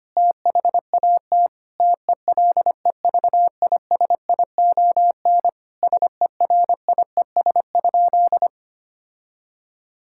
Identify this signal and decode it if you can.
Morse code, heard as a keyed tone, about 25 wpm, 700 Hz. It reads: THAT TELEVISION SERIES?
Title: Sentences from Top 1000 Words - Review Part 5 - 25wpm